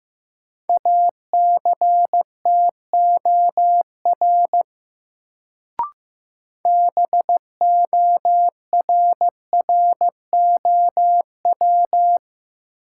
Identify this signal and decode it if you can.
Morse code, heard as a keyed tone, about 16 wpm, 700 Hz. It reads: ACTOR BORROW
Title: Sentences from Top 1500 Words - New Word Drill - 15wpm